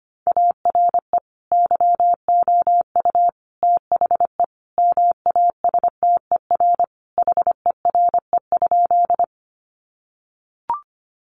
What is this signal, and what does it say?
Morse code, heard as a keyed tone, about 25 wpm, 700 Hz. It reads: ARE YOU THE MASTER HERE?
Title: Sentences from Top 1300 Words 25wpm